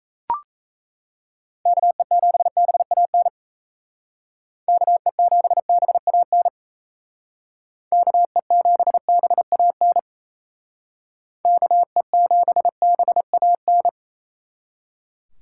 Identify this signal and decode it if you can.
Morse code, heard as a keyed tone, about 35 wpm, 700 Hz. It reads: KE7BAN KE7BAN KE7BAN KE7BAN
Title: Speed Racing - USA 2x3 Call Signs 28wpm